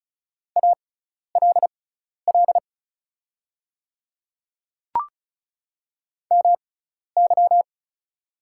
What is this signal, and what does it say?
Morse code, heard as a keyed tone, about 36 wpm, spaced as for 12 wpm, 700 Hz. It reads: ALL MY
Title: 6x Character Spacing - Top 100 Words - 35wpm